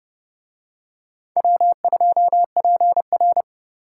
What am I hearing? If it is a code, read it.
Morse code, heard as a keyed tone, about 26 wpm, 700 Hz. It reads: W2PR